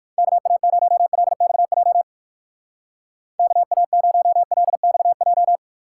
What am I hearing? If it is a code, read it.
Morse code, heard as a keyed tone, about 59 wpm, 700 Hz. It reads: KA0LXJ KA0LXJ